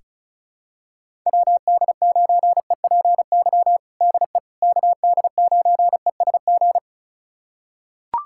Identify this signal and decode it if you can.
Morse code, heard as a keyed tone, about 34 wpm, 700 Hz. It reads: WD9EPY DE KD9ESG
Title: USA Callsign DE Callsign 35wpm